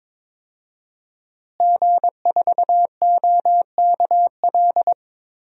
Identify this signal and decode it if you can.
Morse code, heard as a keyed tone, about 22 wpm, 700 Hz. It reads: G4OKL
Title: DXCC-100 Call Signs - 5 Characters - Repeated 3x - 22wpm